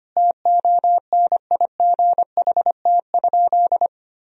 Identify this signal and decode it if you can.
Morse code, heard as a keyed tone, about 25 wpm, 700 Hz. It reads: TONIGHT?
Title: Sentences from Top 1000 Words - Review Part 2 - 25wpm